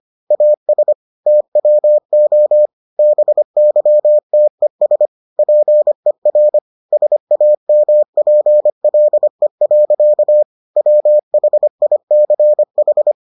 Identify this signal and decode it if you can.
Morse code, heard as a keyed tone, about 25 wpm, 600 Hz. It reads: AS TWO BYTES PER SAMPLE. WHICH